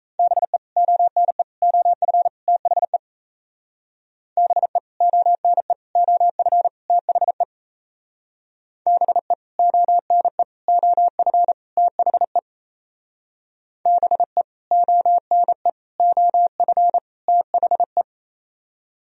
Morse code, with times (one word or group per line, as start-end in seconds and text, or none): 0.2-0.6 BE
0.8-1.4 ONE
1.6-2.3 OF
2.5-3.0 THE
4.4-4.8 BE
5.0-5.7 ONE
6.0-6.7 OF
6.9-7.4 THE
8.9-9.3 BE
9.6-10.4 ONE
10.7-11.5 OF
11.8-12.4 THE
13.9-14.4 BE
14.7-15.7 ONE
16.0-17.0 OF
17.3-18.0 THE